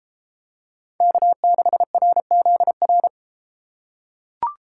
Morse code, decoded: K6RZR

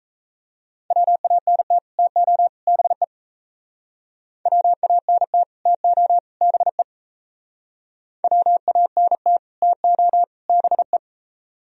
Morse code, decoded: WANT TO BE WANT TO BE WANT TO BE

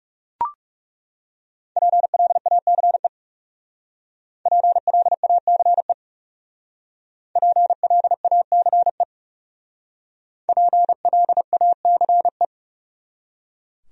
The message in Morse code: PLACE PLACE PLACE PLACE

